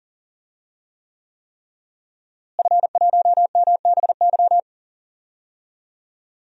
Morse code, heard as a keyed tone, about 39 wpm, 700 Hz. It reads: F1MBY
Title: DXCC-100 Call Signs - 5 Characters - Repeated 3x - 40wpm